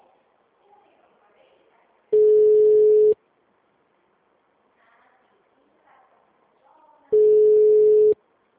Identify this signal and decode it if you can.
Ringback tone (Europe, 425 Hz), with speech.